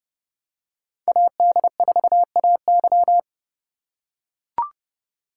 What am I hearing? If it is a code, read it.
Morse code, heard as a keyed tone, about 29 wpm, 700 Hz. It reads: AD4AY